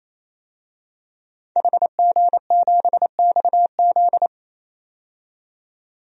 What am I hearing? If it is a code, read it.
Morse code, heard as a keyed tone, about 27 wpm, 700 Hz. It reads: HG7XZ
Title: DXCC-100 Call Signs - 5 Characters - 28wpm